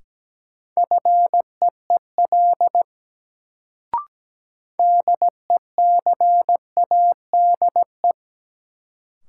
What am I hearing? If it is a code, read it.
Morse code, heard as a keyed tone, about 17 wpm, 700 Hz. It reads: FEEL DECADE
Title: Sets of 5 Words - New Word Drill - 17wpm